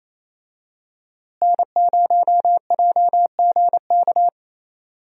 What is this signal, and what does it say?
Morse code, heard as a keyed tone, about 28 wpm, 700 Hz. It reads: N0JGK
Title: USA Call Signs - 5 Characters - Repeated 2x - 28wpm